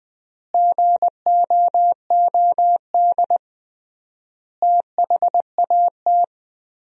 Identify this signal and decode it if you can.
Morse code, heard as a keyed tone, about 20 wpm, 700 Hz. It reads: GOOD THAT
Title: 3x Word-Spacing - Sentences from Top 200 Words - 20wpm